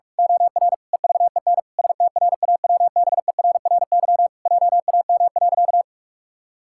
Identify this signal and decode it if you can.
Morse code, heard as a keyed tone, about 46 wpm, 700 Hz. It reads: OR EVEN STRAWBERRY JAM.
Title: Pangram Challenge v2 - 45wpm